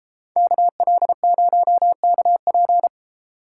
Morse code, heard as a keyed tone, about 30 wpm, 700 Hz. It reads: KL0KP